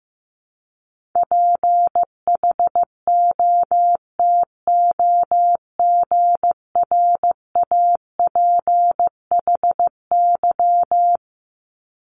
Morse code, heard as a keyed tone, about 15 wpm, 700 Hz. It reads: PHOTOGRAPHY
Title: Multi-letter Phonemes Repeated 3x 15wpm